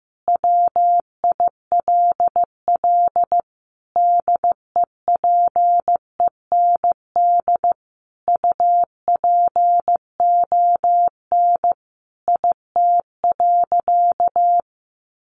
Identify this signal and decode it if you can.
Morse code, heard as a keyed tone, about 15 wpm, 700 Hz. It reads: WILL DEPEND UPON IT.
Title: Sentences from Top 1300 Words 15wpm